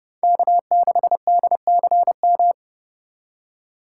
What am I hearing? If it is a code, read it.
Morse code, heard as a keyed tone, about 31 wpm, 700 Hz. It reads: K6DCM